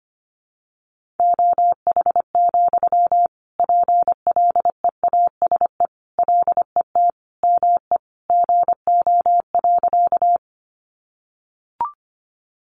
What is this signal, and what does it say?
Morse code, heard as a keyed tone, about 25 wpm, 700 Hz. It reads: OH, PLEASE LET ME GO.
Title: Sentences from Top 1500 Words 25wpm